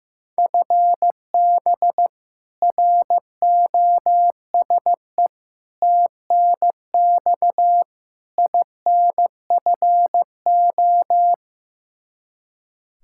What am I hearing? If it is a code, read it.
Morse code, heard as a keyed tone, about 15 wpm, 700 Hz. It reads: FB ROSE TNX INFO
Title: Practice QSO - Exchange 3 - More Info - 15wpm